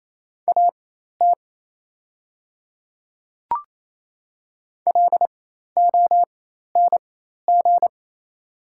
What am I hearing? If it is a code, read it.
Morse code, heard as a keyed tone, about 28 wpm, spaced as for 13 wpm, 700 Hz. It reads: AT LONG
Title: 4x Character Spacing - Top 100 Words - 28wpm